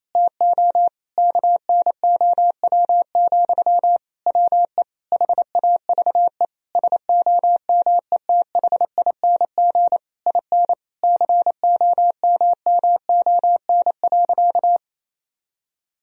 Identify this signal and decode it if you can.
Morse code, heard as a keyed tone, about 28 wpm, 700 Hz. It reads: TO KNOW, WE HAVE SOMETHING IN COMMON.
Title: Sentences from Top 1000 Words - Review Part 3 - 28wpm